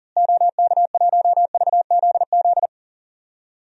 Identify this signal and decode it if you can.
Morse code, heard as a keyed tone, about 40 wpm, 700 Hz. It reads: OK1VZZ
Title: DXCC-100 Call Signs - 6 Characters - Repeated 2x - 40wpm